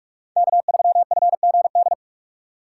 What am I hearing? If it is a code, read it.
Morse code, heard as a keyed tone, about 46 wpm, 700 Hz. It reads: K3FGD
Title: USA Call Signs - 3x Character Spacing - 45wpm